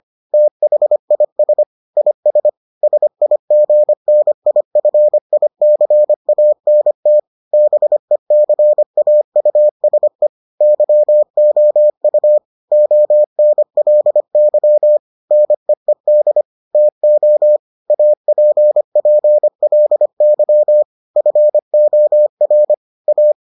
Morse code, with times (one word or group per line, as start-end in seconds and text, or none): 0.3-1.6 THIS
2.0-2.5 IS
2.8-7.2 SIGNIFICANT
7.5-10.3 BECAUSE
10.6-12.4 YOU
12.7-15.0 ONLY
15.3-16.4 NEED
16.8-17.6 TO
17.9-20.8 APPLY
21.2-22.7 FOR
23.1-23.3 A